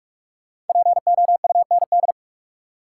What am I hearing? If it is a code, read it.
Morse code, heard as a keyed tone, about 45 wpm, 700 Hz. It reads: POUND
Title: Sentences from Top 1000 Words - Review Part 2 - New Word Drill - 45wpm